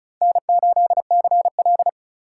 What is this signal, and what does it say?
Morse code, heard as a keyed tone, about 43 wpm, 700 Hz. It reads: N8CL
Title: Speed Racing - USA Call Signs 35wpm